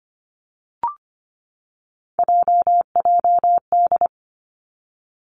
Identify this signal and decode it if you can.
Morse code, heard as a keyed tone, about 25 wpm, 700 Hz. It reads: JJD
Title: Three Challenging Letter-Number Combinations 25wpm